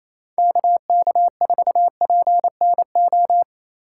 Morse code, heard as a keyed tone, about 27 wpm, 700 Hz. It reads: KK4PNO
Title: USA Call Signs - 6 Characters - 3x Character Spacing - 28wpm